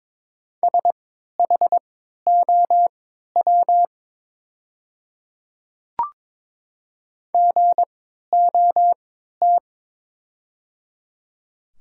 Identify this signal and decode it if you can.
Morse code, heard as a keyed tone, about 23 wpm, spaced as for 13 wpm, 700 Hz. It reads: SHOW GOT
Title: 3x Character Spacing - Top 500 Words - 22wpm